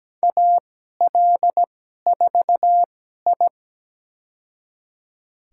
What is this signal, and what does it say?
Morse code, heard as a keyed tone, about 17 wpm, spaced as for 12 wpm, 700 Hz. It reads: AL4I
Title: USA Call Signs - 4 Characters - 2x Character Spacing - 17wpm